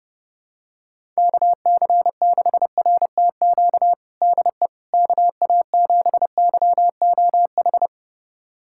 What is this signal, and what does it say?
Morse code, heard as a keyed tone, about 29 wpm, 700 Hz. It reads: KC6RTQ DE KA7YOH
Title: USA Callsign DE Callsign - 6 Characters - 30wpm